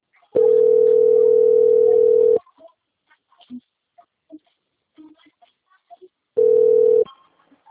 Ringback tone (North America), 440 + 480 Hz.
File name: out- - - - - .44525